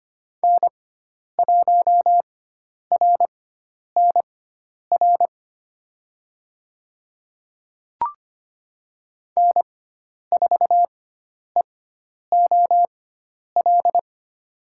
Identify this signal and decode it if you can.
Morse code, heard as a keyed tone, about 25 wpm, spaced as for 10 wpm, 700 Hz. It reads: N1RNR N4EOL
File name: Lesson 16g - Letter c - Call Signs 25wpm